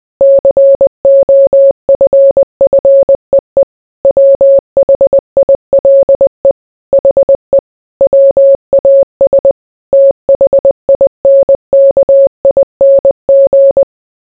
Morse code, heard as a keyed tone, about 20 wpm, 550 Hz. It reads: COFFEE WHILE HE WAS THINKING